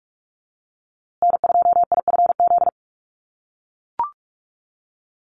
Morse code, heard as a keyed tone, about 44 wpm, 700 Hz. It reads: N2IFZ
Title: USA Call Signs Repeated 3x 45wpm